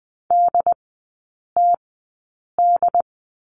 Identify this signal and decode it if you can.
Morse code, heard as a keyed tone, about 20 wpm, spaced as for 8 wpm, 700 Hz. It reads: DTD
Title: 067 Lesson 12e - Letter d - Three Characters 20wpm